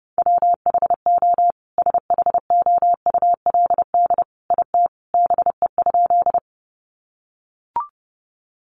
Morse code, heard as a keyed tone, about 30 wpm, 700 Hz. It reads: WHO SHOULD IT BE?
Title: Sentences from Top 300 Words 30wpm